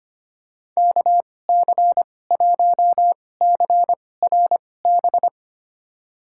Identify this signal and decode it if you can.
Morse code, heard as a keyed tone, about 25 wpm, spaced as for 18 wpm, 700 Hz. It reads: KC1CRB
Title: USA Call Signs - 6 Characters - 2x Character Spacing - 25wpm